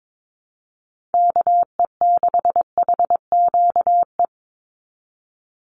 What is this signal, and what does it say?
Morse code, heard as a keyed tone, about 22 wpm, 700 Hz. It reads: KE6HQE
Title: USA Call Signs 22wpm